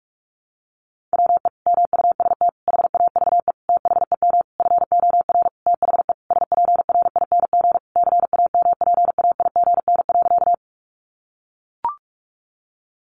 Morse code, heard as a keyed tone, about 44 wpm, 700 Hz. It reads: WE MUST HAVE THEM FOR THE SPRING CAMPAIGN.
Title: Sentences from Top 1200 Words 45wpm